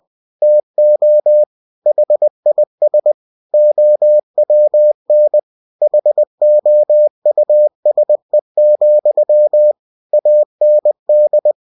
Morse code, heard as a keyed tone, about 20 wpm, 600 Hz. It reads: TO HIS OWN HOUSE, AND